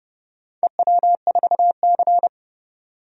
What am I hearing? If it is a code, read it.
Morse code, heard as a keyed tone, about 30 wpm, 700 Hz. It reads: EW4C
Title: DXCC-100 Call Signs - 4 Characters - 30wpm